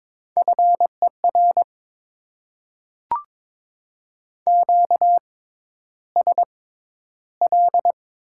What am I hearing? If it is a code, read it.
Morse code, heard as a keyed tone, about 22 wpm, spaced as for 11 wpm, 700 Hz. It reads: FER QSL
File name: Top 90 QSO Elements - 6x Character Spacing - 22wpm